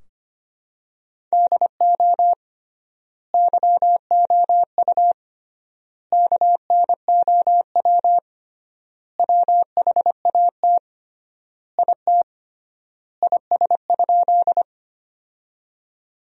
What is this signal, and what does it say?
Morse code, heard as a keyed tone, about 25 wpm, 700 Hz. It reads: DO YOU KNOW WHAT IT IS?
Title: 3x Word-Spacing - Sentences from Top 100 Words - 25wpm